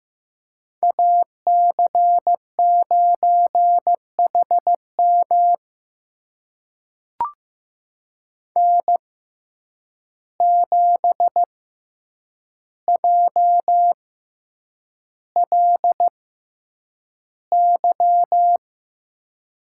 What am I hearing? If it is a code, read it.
Morse code, heard as a keyed tone, about 15 wpm, spaced as for 8 wpm, 700 Hz. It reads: AC9HM N7JLY
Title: USA Call Signs - 5 Characters - 6x Character Spacing - 15wpm